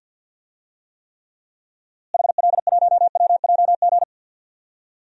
Morse code, heard as a keyed tone, about 49 wpm, 700 Hz. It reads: HL1WJG
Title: DXCC-100 Call Signs - 6 Characters - Repeated 2x - 50wpm